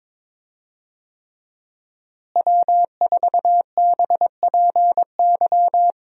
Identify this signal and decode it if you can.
Morse code, heard as a keyed tone, about 22 wpm, 700 Hz. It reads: W4BPY
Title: USA Call Signs - 5 Characters - Repeated 3x - 22wpm